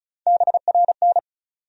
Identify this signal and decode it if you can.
Morse code, heard as a keyed tone, about 35 wpm, 700 Hz. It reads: BRN
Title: USA Call Sign Postfixes - 3 Letters 35wpm